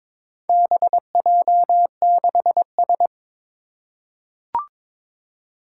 Morse code, heard as a keyed tone, about 22 wpm, 700 Hz. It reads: BJ6S
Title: Four Challenging Letter-Number Combinations 22wpm